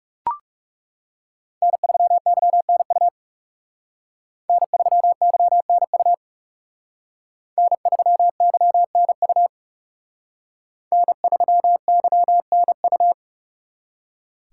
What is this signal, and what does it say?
Morse code, heard as a keyed tone, about 36 wpm, 700 Hz. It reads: N3YNU N3YNU N3YNU N3YNU
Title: Speed Racing - USA 1x3 Call Signs 30wpm